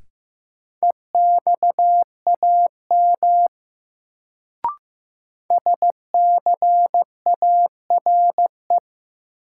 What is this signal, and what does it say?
Morse code, heard as a keyed tone, about 15 wpm, 700 Hz. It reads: EXAM SCARE
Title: Sentences from Top 1000 Words - Review Part 5 - New Word Drill - 15wpm